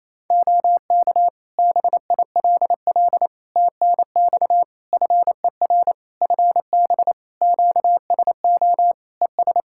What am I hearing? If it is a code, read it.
Morse code, heard as a keyed tone, about 27 wpm, 700 Hz. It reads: OK BILL TNX FER FB QSO ES
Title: Practice QSO - Exchange 4 - Ending - 28wpm